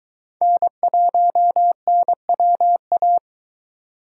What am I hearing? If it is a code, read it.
Morse code, heard as a keyed tone, about 22 wpm, 700 Hz. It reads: N1NWA